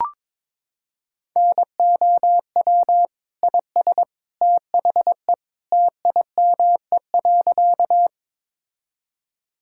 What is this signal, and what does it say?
Morse code, heard as a keyed tone, about 22 wpm, 700 Hz. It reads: NOW IS THE TIME.
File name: practice / 22wpm / Sentences with 2-Word Phrases - 22wpm